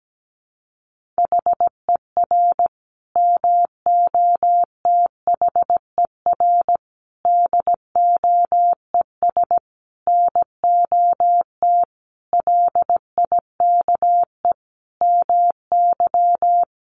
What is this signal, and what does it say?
Morse code, heard as a keyed tone, about 17 wpm, 700 Hz. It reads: HER MOTHER DOES NOT LIKE MY
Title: Sentences from Top 1400 Words 17wpm